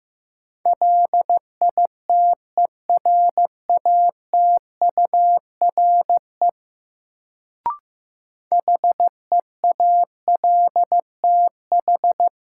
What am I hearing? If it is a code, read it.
Morse code, heard as a keyed tone, about 15 wpm, 700 Hz. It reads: LITERATURE HEALTH
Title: Sentences from Top 600 Words - New Word Drill - 15wpm